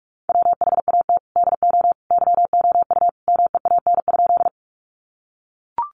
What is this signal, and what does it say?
Morse code, heard as a keyed tone, about 44 wpm, 700 Hz. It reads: WHAT DO YOU MEAN?
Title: Sentences from Top 200 Words 45wpm